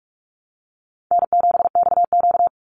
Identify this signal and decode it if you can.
Morse code, heard as a keyed tone, about 44 wpm, 700 Hz. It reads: N7XQ
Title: USA Call Signs 45wpm